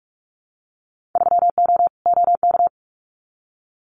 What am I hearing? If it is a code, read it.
Morse code, heard as a keyed tone, about 44 wpm, 700 Hz. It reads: 3O OK
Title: Winter Field Day Exchanges 45wpm